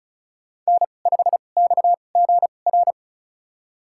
Morse code, heard as a keyed tone, about 36 wpm, spaced as for 25 wpm, 700 Hz. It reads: N5XGR
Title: USA Call Signs - 5 Characters - 2x Character Spacing - 35wpm